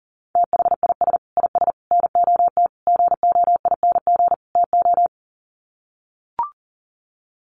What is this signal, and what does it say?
Morse code, heard as a keyed tone, about 40 wpm, 700 Hz. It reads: THIS IS NOT GOING TO
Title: Sets of 5 Words Encore2 40wpm